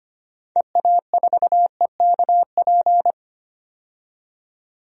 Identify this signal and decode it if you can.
Morse code, heard as a keyed tone, about 25 wpm, 700 Hz. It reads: EA4EKP
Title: DXCC-100 Call Signs - 6 Characters - 25wpm